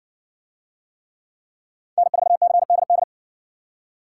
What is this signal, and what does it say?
Morse code, heard as a keyed tone, about 55 wpm, 700 Hz. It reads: N4CDD